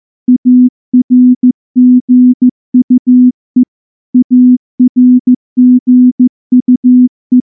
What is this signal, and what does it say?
Morse code, heard as a keyed tone, about 14 wpm, 250 Hz. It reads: ARGUE ARGUE
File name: morse / morse-training-15WPM